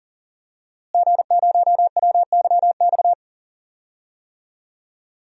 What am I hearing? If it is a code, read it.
Morse code, heard as a keyed tone, about 40 wpm, 700 Hz. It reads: G0WYX
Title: DXCC-100 Call Signs - 5 Characters - Repeated 2x - 40wpm